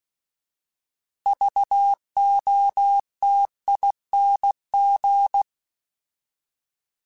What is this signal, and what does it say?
Morse code, heard as a keyed tone, about 16 wpm, 800 Hz. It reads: VOTING